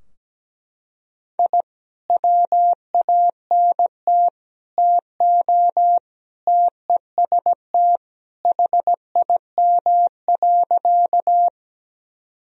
Morse code, heard as a keyed tone, about 17 wpm, 700 Hz. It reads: I WANT TO TEST HIM.
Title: Sentences from Top 1000 Words - Review Part 3 - 17wpm